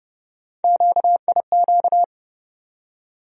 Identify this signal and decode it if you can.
Morse code, heard as a keyed tone, about 30 wpm, 700 Hz. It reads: QIQ
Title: USA Call Sign Postfixes - 3 Letters 30wpm